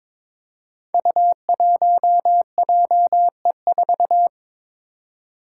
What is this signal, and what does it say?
Morse code, heard as a keyed tone, about 22 wpm, 700 Hz. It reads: U1JE4
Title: Five Challenging Letter-Number Combinations 22wpm